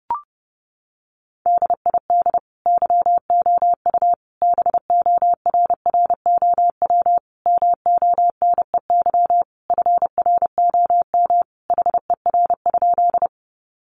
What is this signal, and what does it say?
Morse code, heard as a keyed tone, about 30 wpm, 700 Hz. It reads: DID YOU BORROW MONEY FROM HER?
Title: Sentences from Top 1500 Words 30wpm